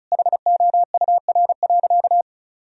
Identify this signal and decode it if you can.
Morse code, heard as a keyed tone, about 36 wpm, 700 Hz. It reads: HOUR.
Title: Sentences from Top 1000 Words - Review Part 4 - 35wpm